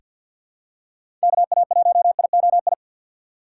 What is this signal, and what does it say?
Morse code, heard as a keyed tone, about 46 wpm, 700 Hz. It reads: KA1IOI
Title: USA Call Signs - 6 Characters - 6x Character Spacing - 50wpm